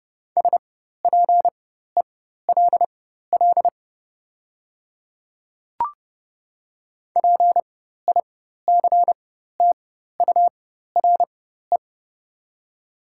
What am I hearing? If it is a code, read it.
Morse code, heard as a keyed tone, about 31 wpm, spaced as for 14 wpm, 700 Hz. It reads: SPELL PICTURE
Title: 4x Character Spacing - Top 200 Words - 30wpm